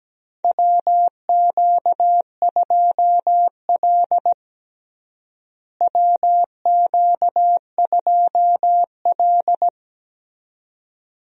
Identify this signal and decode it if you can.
Morse code, heard as a keyed tone, about 17 wpm, 700 Hz. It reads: WQ2L WQ2L
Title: USA Call Signs - 4 Characters - Repeated 2x - 17wpm